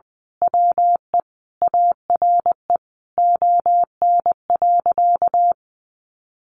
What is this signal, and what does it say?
Morse code, heard as a keyed tone, about 20 wpm, 700 Hz. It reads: WE ARE ON.
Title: Sentences from Top 1100 Words 20wpm